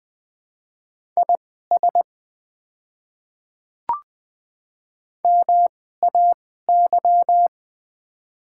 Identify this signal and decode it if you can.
Morse code, heard as a keyed tone, about 20 wpm, spaced as for 14 wpm, 700 Hz. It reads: IS MAY
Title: 2x Character Spacing - Top 100 Words - 20wpm